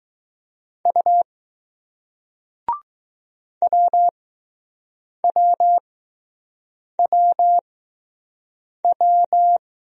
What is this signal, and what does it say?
Morse code, heard as a keyed tone, about 20 wpm, spaced as for 6 wpm, 700 Hz. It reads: U WWWW